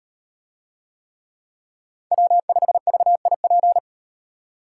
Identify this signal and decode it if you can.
Morse code, heard as a keyed tone, about 35 wpm, 700 Hz. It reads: W5VIP